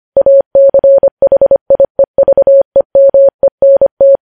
Morse code, heard as a keyed tone, about 25 wpm, 550 Hz. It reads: ACHIEVEMENT
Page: news headlines in morse code at 25 wpm